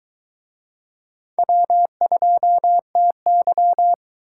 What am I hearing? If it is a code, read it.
Morse code, heard as a keyed tone, about 22 wpm, 700 Hz. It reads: W2TY